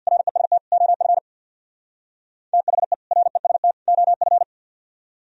Morse code, heard as a keyed tone, about 60 wpm, 700 Hz. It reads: REST OF THE REST OF
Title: Speed-Racing - Sets of 3 Words 50wpm